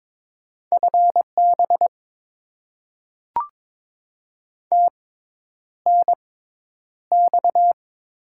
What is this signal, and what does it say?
Morse code, heard as a keyed tone, about 22 wpm, spaced as for 10 wpm, 700 Hz. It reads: FB TNX
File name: Top 20 QSO Elements - 6x Character Spacing - 22wpm